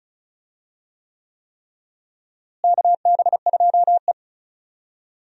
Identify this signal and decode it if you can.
Morse code, heard as a keyed tone, about 34 wpm, 700 Hz. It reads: KB2E
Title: USA Call Signs - 4 Characters - Repeated 3x - 35wpm